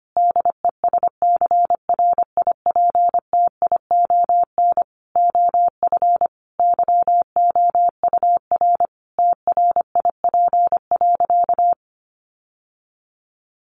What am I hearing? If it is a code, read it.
Morse code, heard as a keyed tone, about 25 wpm, 700 Hz. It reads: DESCRIPTION OF YOUR TRIP.
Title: Sentences from Top 1500 Words 25wpm